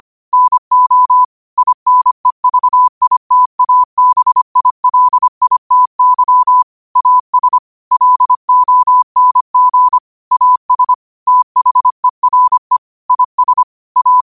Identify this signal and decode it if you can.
Morse code, heard as a keyed tone, about 25 wpm, 1000 Hz. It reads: NO INEVITABILITY AS LONG AS THERE IS A
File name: qotdfiles / 25wpm